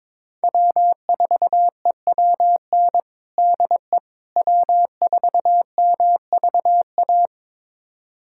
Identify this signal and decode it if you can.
Morse code, heard as a keyed tone, about 22 wpm, 700 Hz. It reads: W4EWN DE W4MVA
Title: USA Callsign DE Callsign - 5 Characters - 22wpm